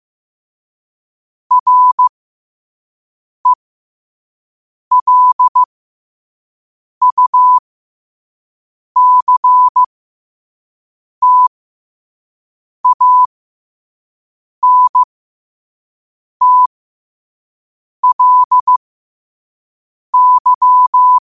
Morse code, heard as a keyed tone, about 15 wpm, spaced as for 5 wpm, 1000 Hz. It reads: RELUCTANTLY
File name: qotdfiles / 5wpm